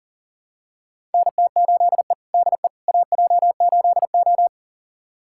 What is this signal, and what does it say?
Morse code, heard as a keyed tone, about 37 wpm, 700 Hz. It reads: NT8E DE AJ8O
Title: USA Callsign DE Callsign - 4 Characters - 40wpm